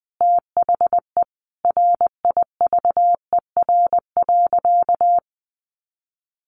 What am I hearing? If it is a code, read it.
Morse code, heard as a keyed tone, about 20 wpm, 700 Hz. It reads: THE RIVER.
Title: Sentences from Top 1300 Words 20wpm